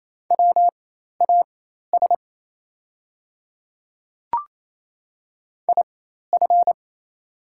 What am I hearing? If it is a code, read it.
Morse code, heard as a keyed tone, about 28 wpm, spaced as for 13 wpm, 700 Hz. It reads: WAS IF
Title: 4x Character Spacing - Top 100 Words - 28wpm